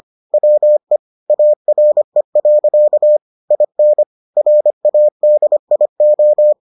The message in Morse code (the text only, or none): WE ARE. IN RADIO